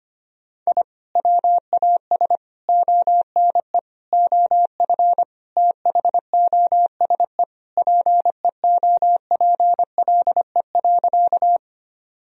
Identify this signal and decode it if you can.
Morse code, heard as a keyed tone, about 25 wpm, 700 Hz. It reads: I WAS ONE OF THOSE PEOPLE.
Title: Sentences from Top 1000 Words - Review Part 1 - 25wpm